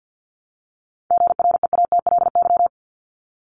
Morse code, heard as a keyed tone, about 50 wpm, 700 Hz. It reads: GREATLY
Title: Top 1251-1500 Words 50wpm